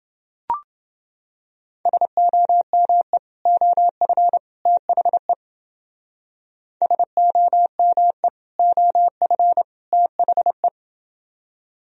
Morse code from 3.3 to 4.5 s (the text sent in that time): OF